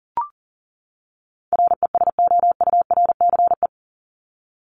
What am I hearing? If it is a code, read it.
Morse code, heard as a keyed tone, about 39 wpm, 700 Hz. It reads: RESOURCE
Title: Top 1001-1250 Words 40wpm